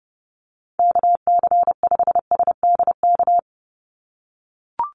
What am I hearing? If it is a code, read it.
Morse code, heard as a keyed tone, about 28 wpm, 700 Hz. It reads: KC5SDK
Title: USA Call Signs 30wpm